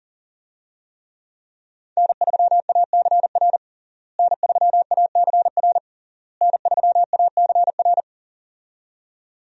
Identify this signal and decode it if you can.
Morse code, heard as a keyed tone, about 40 wpm, 700 Hz. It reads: N3ACR N3ACR N3ACR
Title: USA Call Signs - 5 Characters - Repeated 3x - 40wpm